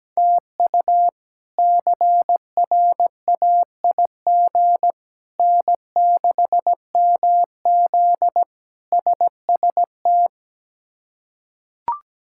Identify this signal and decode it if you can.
Morse code, heard as a keyed tone, about 17 wpm, 700 Hz. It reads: TU CRAIG N6MZ SST
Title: K1USN SST QSO Practice - 17wpm